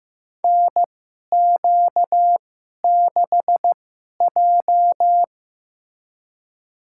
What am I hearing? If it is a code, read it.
Morse code, heard as a keyed tone, about 15 wpm, spaced as for 11 wpm, 700 Hz. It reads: NQ6J
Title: USA Call Signs - 4 Characters - 2x Character Spacing - 15wpm